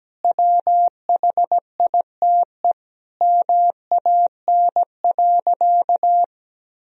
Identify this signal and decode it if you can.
Morse code, heard as a keyed tone, about 17 wpm, 700 Hz. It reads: WHITE MAN.
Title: Sentences from Top 1000 Words - Review Part 1 - 17wpm